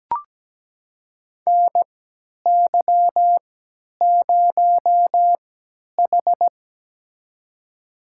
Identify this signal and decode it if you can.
Morse code, heard as a keyed tone, about 17 wpm, spaced as for 10 wpm, 700 Hz. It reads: NY0H
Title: USA Call Signs - 4 Characters - 3x Character Spacing - 17wpm